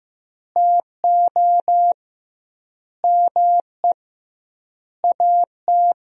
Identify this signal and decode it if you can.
Morse code, heard as a keyed tone, about 15 wpm, 700 Hz. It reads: TO ME AT
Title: 2x Word-Spacing - Sentences from Top 500 Words - 15wpm